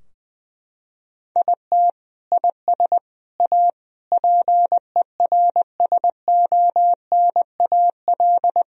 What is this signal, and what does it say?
Morse code, heard as a keyed tone, about 20 wpm, 700 Hz. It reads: IT IS A PERSONAL
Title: Sentences from Top 1000 Words - Review Part 4 - 20wpm